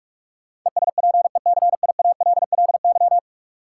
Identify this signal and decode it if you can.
Morse code, heard as a keyed tone, about 44 wpm, 700 Hz. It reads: ESPECIALLY
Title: Sentences from Top 700 Words - New Word Drill - 45wpm